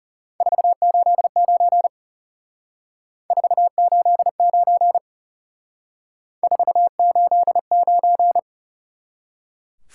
Morse code, heard as a keyed tone, about 36 wpm, 700 Hz. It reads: 489 489 489
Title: Speed-Racing Top 180 QSO Elements 30wpm